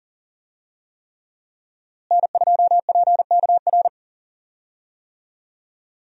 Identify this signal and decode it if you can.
Morse code, heard as a keyed tone, about 40 wpm, 700 Hz. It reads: N2PKR
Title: USA Call Signs - 5 Characters - Repeated 3x - 40wpm